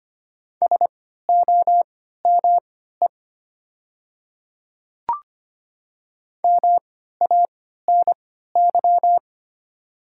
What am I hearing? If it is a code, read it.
Morse code, heard as a keyed tone, about 25 wpm, spaced as for 14 wpm, 700 Hz. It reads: SOME MANY
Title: 3x Character Spacing - Top 200 Words - 25wpm